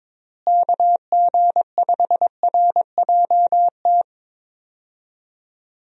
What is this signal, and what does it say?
Morse code, heard as a keyed tone, about 21 wpm, 700 Hz. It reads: KG5RJT
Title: USA Call Signs - 6 Characters - Repeated 2x - 22wpm